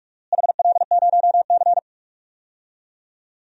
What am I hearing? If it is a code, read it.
Morse code, heard as a keyed tone, about 46 wpm, 700 Hz. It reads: HL0C
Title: DXCC-100 Call Signs - 4 Characters - Repeated 2x - 45wpm